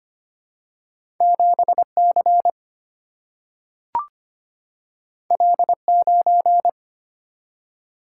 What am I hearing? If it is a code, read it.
Morse code, heard as a keyed tone, about 25 wpm, 700 Hz. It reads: 7C L9
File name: Two Letter-Number Combinations - 25wpm